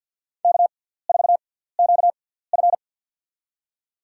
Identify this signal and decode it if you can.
Morse code, heard as a keyed tone, about 47 wpm, spaced as for 17 wpm, 700 Hz. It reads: K4QF
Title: USA Call Signs - 6x Character Spacing - 50wpm